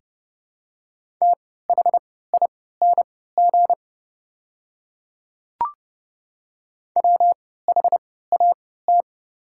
Morse code, heard as a keyed tone, about 30 wpm, spaced as for 17 wpm, 700 Hz. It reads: THING WHAT
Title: 3x Character Spacing - Top 200 Words - 30wpm